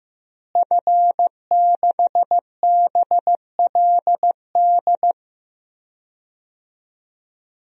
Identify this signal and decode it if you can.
Morse code, heard as a keyed tone, about 15 wpm, 700 Hz. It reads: F6BLD
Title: DXCC-100 Call Signs - 5 Characters - Repeated 2x - 15wpm